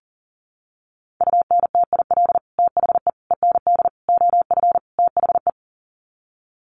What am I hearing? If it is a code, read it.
Morse code, heard as a keyed tone, about 40 wpm, 700 Hz. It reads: UNTIL THE END OF THE
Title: Sets of 5 Words Encore 40wpm